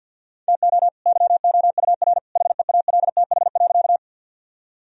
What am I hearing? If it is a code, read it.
Morse code, heard as a keyed tone, about 46 wpm, 700 Hz. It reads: TO YOUR HEALTH.
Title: Sentences from Top 1000 Words - Review Part 3 - 50wpm